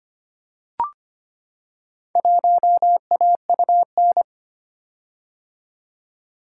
Morse code, heard as a keyed tone, about 25 wpm, 690 Hz. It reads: 1AUN
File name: practice / 25wpm / Four Letter-Number Combinations - 25wpm